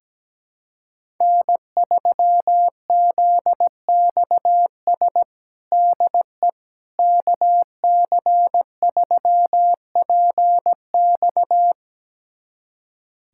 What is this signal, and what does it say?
Morse code, heard as a keyed tone, about 17 wpm, 700 Hz. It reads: N3ZXS DE KC3PX
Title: USA Callsign DE Callsign - 5 Characters - 17wpm